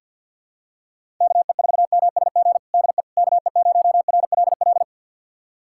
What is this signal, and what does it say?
Morse code, heard as a keyed tone, about 52 wpm, 700 Hz. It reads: KE4MSG DE KE0RLL
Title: USA Callsign DE Callsign - 6 Characters - 50wpm